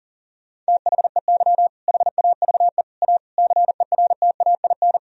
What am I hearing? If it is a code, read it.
Morse code, heard as a keyed tone, about 40 wpm, 700 Hz. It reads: THEY HAVE A CERTAIN
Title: Sentences from Top 1000 Words - Review Part 4 - 40wpm